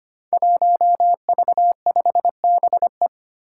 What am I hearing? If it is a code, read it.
Morse code, heard as a keyed tone, about 25 wpm, 700 Hz. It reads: 1V5BE
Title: Five Challenging Letter-Number Combinations 25wpm